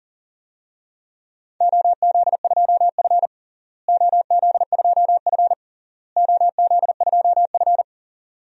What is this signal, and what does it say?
Morse code, heard as a keyed tone, about 40 wpm, 700 Hz. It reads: OZ2F OZ2F OZ2F
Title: DXCC-100 Call Signs - 4 Characters - Repeated 3x - 40wpm